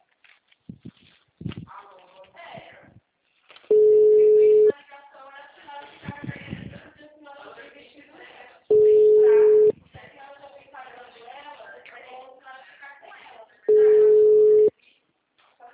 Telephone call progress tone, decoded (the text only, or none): ringback tone